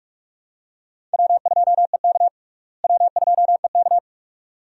Morse code, heard as a keyed tone, about 43 wpm, 700 Hz. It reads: W2EK W2EK
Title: USA Call Signs - 4 Characters - Repeated 2x - 45wpm